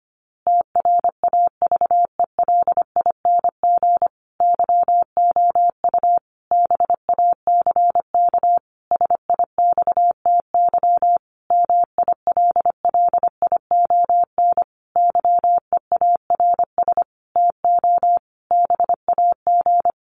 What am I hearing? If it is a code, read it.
Morse code, heard as a keyed tone, about 25 wpm, 700 Hz. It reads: TRAVELING YOU BACK SIXTY MILLION YEARS TO BAG
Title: A Sound Of Thunder 25wpm